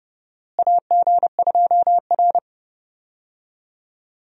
Morse code, heard as a keyed tone, about 31 wpm, 700 Hz. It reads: AG2R
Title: USA Call Signs - 4 Characters - Repeated 3x - 30wpm